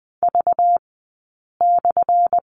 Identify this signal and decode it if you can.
Morse code, heard as a keyed tone, about 20 wpm, spaced as for 8 wpm, 700 Hz. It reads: V/
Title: 204 Lesson 33d - Letter j - Two Characters 20wpm